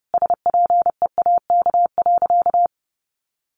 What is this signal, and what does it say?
Morse code, heard as a keyed tone, about 30 wpm, 700 Hz. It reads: SPEAK.
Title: Sentences from Top 1300 Words 30wpm